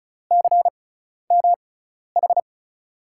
Morse code, heard as a keyed tone, about 36 wpm, spaced as for 12 wpm, 700 Hz. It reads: CMH
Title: USA Call Signs - 5 Characters - 6x Character Spacing - 35wpm